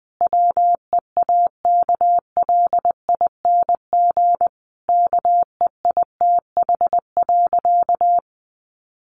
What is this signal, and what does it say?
Morse code, heard as a keyed tone, about 20 wpm, 700 Hz. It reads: WEAKLING KEITH.
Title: A Sound Of Thunder 20wpm